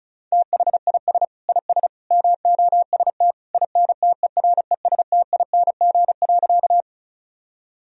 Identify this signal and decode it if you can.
Morse code, heard as a keyed tone, about 34 wpm, 700 Hz. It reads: THIS IS MOST INTERESTING.
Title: Sentences from Top 1000 Words - Review Part 1 - 35wpm